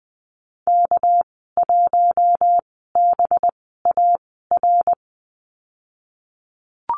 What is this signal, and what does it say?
Morse code, heard as a keyed tone, about 20 wpm, spaced as for 15 wpm, 700 Hz. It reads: K1BAR
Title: 200 Lesson 32g - Letter k - Call Signs 20wpm